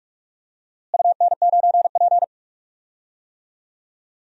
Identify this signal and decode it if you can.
Morse code, heard as a keyed tone, about 46 wpm, 700 Hz. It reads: UN9P
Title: DXCC-100 Call Signs - 4 Characters - Repeated 2x - 45wpm